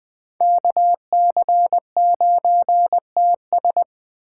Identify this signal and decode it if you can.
Morse code, heard as a keyed tone, about 20 wpm, 700 Hz. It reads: KC9TS
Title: Speed Racing - USA 2x2 Call Signs 17wpm